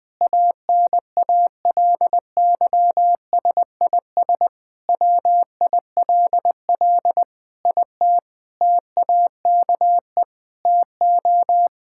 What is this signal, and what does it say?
Morse code, heard as a keyed tone, about 20 wpm, 700 Hz. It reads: ANALYSIS WILL IT TAKE TO
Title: Sentences from Top 1000 Words - Review Part 5 - 20wpm